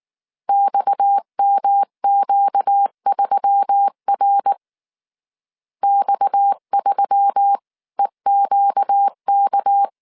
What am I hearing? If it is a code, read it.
Morse code, heard as a keyed tone, about 19 wpm, 800 Hz. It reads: XMQ3R X3EQK
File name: CW / DET / 1 6 1500C S C 18wpm 2356